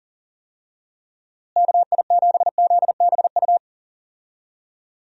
Morse code, heard as a keyed tone, about 40 wpm, 700 Hz. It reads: KI7ZBU
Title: USA Call Signs - 6 Characters - Repeated 3x - 40wpm